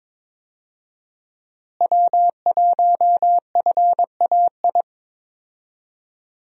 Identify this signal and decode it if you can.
Morse code, heard as a keyed tone, about 22 wpm, 700 Hz. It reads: W1FAI
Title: USA Call Signs - 5 Characters - Repeated 2x - 22wpm